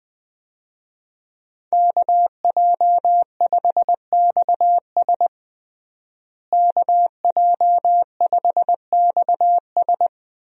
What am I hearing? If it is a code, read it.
Morse code, heard as a keyed tone, about 20 wpm, 700 Hz. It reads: KJ5XS KJ5XS
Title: USA Call Signs - 5 Characters - Repeated 2x - 20wpm